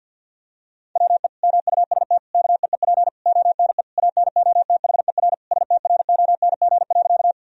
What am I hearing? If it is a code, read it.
Morse code, heard as a keyed tone, about 50 wpm, 700 Hz. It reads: WE MUST KEEP ONE ANOTHER STRONG.
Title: Sentences from Top 1000 Words - Review Part 3 - 50wpm